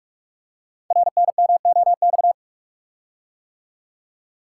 Morse code, heard as a keyed tone, about 44 wpm, 700 Hz. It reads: RNMOX
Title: Code Groups - 5 Letters - 45wpm